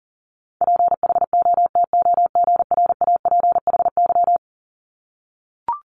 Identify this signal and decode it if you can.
Morse code, heard as a keyed tone, about 40 wpm, 700 Hz. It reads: PHOTOGRAPHY
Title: Multi-letter Phonemes Repeated 3x 40wpm